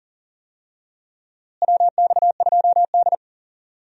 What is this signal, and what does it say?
Morse code, heard as a keyed tone, about 38 wpm, 700 Hz. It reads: WX2D
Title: USA Call Signs - 4 Characters - Repeated 2x - 40wpm